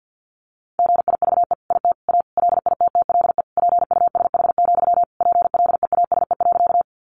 Morse code, heard as a keyed tone, about 50 wpm, 700 Hz. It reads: GIVE IT A LITTLE PUSH, PLEASE.
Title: Sentences from Top 1500 Words 50wpm